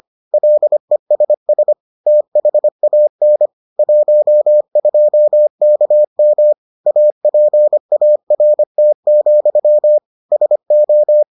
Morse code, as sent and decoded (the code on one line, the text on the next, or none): .-.. ... ... / - .... .- -. / .---- ..--- -.- -- / .- .--. .- .-. - --..-- / ... ---
LESS THAN 12KM APART, SO